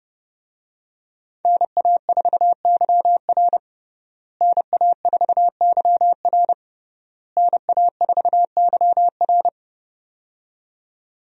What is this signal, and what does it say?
Morse code, heard as a keyed tone, about 30 wpm, 700 Hz. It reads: NA4YR NA4YR NA4YR
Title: USA Call Signs - 5 Characters - Repeated 3x - 30wpm